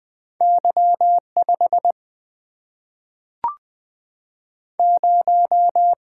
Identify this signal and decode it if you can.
Morse code, heard as a keyed tone, about 20 wpm, 700 Hz. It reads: Y5 0
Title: Single-Letter Single-Number 20wpm